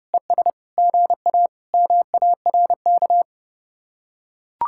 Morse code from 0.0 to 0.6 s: ES

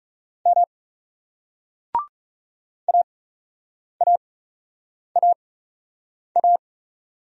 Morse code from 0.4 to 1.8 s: M

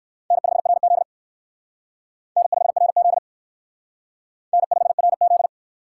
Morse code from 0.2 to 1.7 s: N5RZ

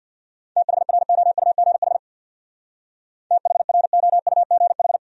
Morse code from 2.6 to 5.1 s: THROUGH

